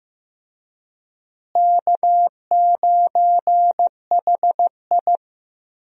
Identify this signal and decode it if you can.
Morse code, heard as a keyed tone, about 15 wpm, 700 Hz. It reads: K9HI
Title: USA Call Signs - 4 Characters - Repeated 2x - 15wpm